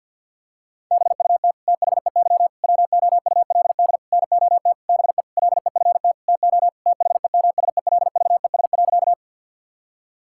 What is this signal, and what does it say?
Morse code, heard as a keyed tone, about 52 wpm, 700 Hz. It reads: BUT THEY WOULD NOT BE LEFT TO THEMSELVES.
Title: Sentences from Top 1000 Words - Review Part 5 - 50wpm